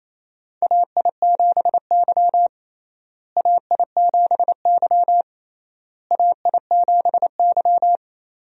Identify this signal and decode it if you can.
Morse code, heard as a keyed tone, about 28 wpm, 700 Hz. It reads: AI7Y AI7Y AI7Y
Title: USA Call Signs - 4 Characters - Repeated 3x - 28wpm